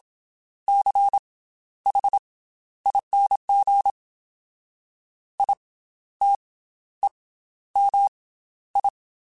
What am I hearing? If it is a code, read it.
Morse code, heard as a keyed tone, about 27 wpm, spaced as for 12 wpm, 800 Hz. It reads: CHING ITEMI
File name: Special INGs / Special INGs-mp3s / Slow ing @25